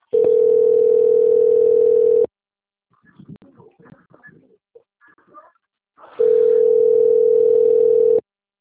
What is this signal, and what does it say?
Ringback tone (North America), 440 + 480 Hz.